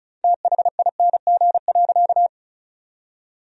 Morse code, heard as a keyed tone, about 34 wpm, 700 Hz. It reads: THING.
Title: 4x Word-Spacing - Sentences from Top 500 Words - 35wpm